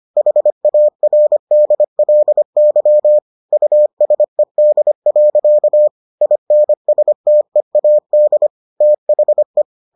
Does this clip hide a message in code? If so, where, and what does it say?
Yes: Morse code from 0.2 to 9.6 s, heard as a keyed tone, about 25 wpm, 600 Hz. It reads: HARDLY USED. INSTEAD THE